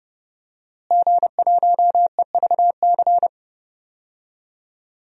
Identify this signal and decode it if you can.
Morse code, heard as a keyed tone, about 30 wpm, 700 Hz. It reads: G1EVC
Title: DXCC-100 Call Signs - 5 Characters - Repeated 3x - 30wpm